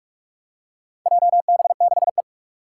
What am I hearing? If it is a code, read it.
Morse code, heard as a keyed tone, about 44 wpm, 700 Hz. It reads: JB6E
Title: Four Challenging Letter-Number Combinations 45wpm